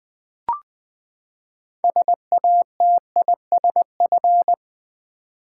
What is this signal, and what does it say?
Morse code, heard as a keyed tone, about 20 wpm, 700 Hz. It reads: SATISF
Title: Build A Word - 6 Letters At A Time - 12-Letter Words - 20wpm